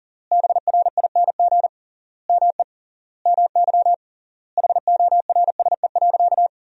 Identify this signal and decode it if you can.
Morse code, heard as a keyed tone, about 37 wpm, 700 Hz. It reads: BRING ME MY HORSE.
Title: 3x Word-Spacing - Sentences from Top 500 Words - 40wpm